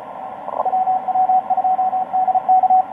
Morse code, heard as a keyed tone, about 35 wpm, 750 Hz. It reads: WW2DK
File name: tx5k dl4ny etc 12m cw 04mar2013 1521